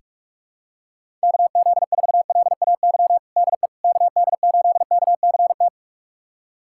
Morse code, heard as a keyed tone, about 46 wpm, 690 Hz. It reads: KZ4LAY DE KD8KCT
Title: USA Callsign DE Callsign - 6 Characters - 45wpm